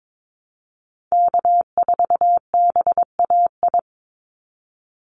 Morse code, heard as a keyed tone, about 22 wpm, 700 Hz. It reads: K4BAI
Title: Call Signs Repeated 3x - 22wpm